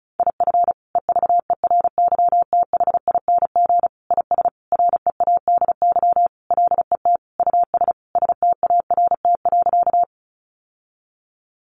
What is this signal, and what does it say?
Morse code, heard as a keyed tone, about 34 wpm, 700 Hz. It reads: IF EVERYTHING IS READY LET US START.
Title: Sentences from Top 700 Words 35wpm